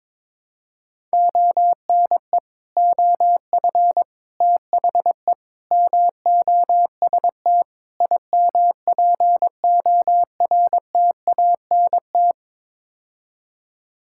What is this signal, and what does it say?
Morse code, heard as a keyed tone, about 22 wpm, 700 Hz. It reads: ONE OF THE MOST IMPORTANT
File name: practice / 22wpm / Build A Phrase - 5-Word Phrases - 22wpm